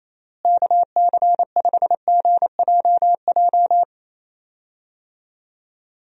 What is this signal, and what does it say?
Morse code, heard as a keyed tone, about 27 wpm, 700 Hz. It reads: KC5GJJ
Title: USA Call Signs - 6 Characters - Repeated 2x - 28wpm